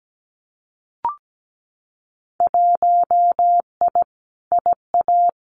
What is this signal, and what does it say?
Morse code, heard as a keyed tone, about 17 wpm, 700 Hz. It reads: 1I IA
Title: Winter Field Day Exchanges 17wpm